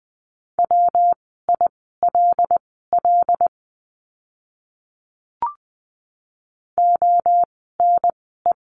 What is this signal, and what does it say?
Morse code, heard as a keyed tone, about 20 wpm, spaced as for 15 wpm, 700 Hz. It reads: WILL ONE
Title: 192 Lesson 31f - Letter v - Words 20wpm